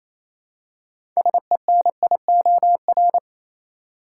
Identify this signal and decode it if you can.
Morse code, heard as a keyed tone, about 28 wpm, 700 Hz. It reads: SENIOR
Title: Six Letter Words - 28wpm